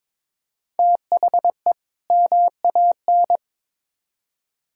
Morse code, heard as a keyed tone, about 23 wpm, 700 Hz. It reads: THE MAN
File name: practice / 22wpm / Build A Phrase - 2-Word Phrases - 22wpm